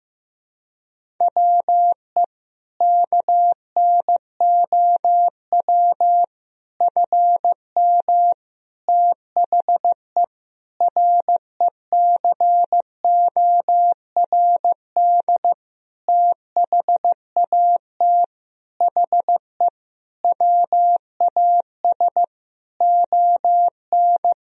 Morse code, heard as a keyed tone, about 16 wpm, 700 Hz. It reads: WE KNOW FM THE RECORD THAT HE WAS ON
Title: Sentences from Top 500 Words and Abbreviations 15wpm